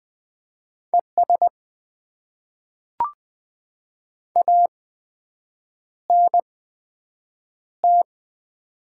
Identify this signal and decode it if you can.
Morse code, heard as a keyed tone, about 20 wpm, spaced as for 7 wpm, 700 Hz. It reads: ES ANT
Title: Top 20 QSO Elements - 8x Character Spacing - 20wpm